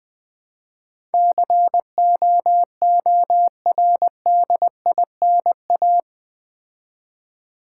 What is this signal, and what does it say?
Morse code, heard as a keyed tone, about 20 wpm, 700 Hz. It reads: COORDINA
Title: Build A Word - 4 Letters At A Time - 12-Letter Words - 20wpm